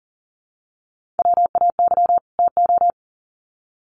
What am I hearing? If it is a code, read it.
Morse code, heard as a keyed tone, about 40 wpm, 700 Hz. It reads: WAY TO
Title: Sets of 2 Words 40wpm